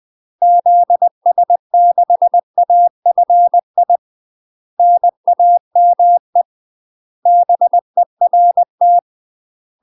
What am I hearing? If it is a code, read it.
Morse code, heard as a keyed tone, about 20 wpm, 700 Hz. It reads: ZS6AFI NAME BERT
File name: UK MASTER QSOs TRAINING / 20WPM / UK MASTER QSOs 0032